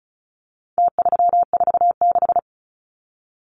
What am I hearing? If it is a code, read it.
Morse code, heard as a keyed tone, about 34 wpm, 700 Hz. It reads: T346
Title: Numbers - Contest Style - 4-Digit 35wpm